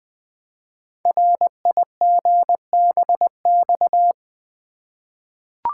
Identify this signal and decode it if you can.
Morse code, heard as a keyed tone, about 20 wpm, 700 Hz. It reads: RIGBX
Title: Code Groups - 5 Letters - 20wpm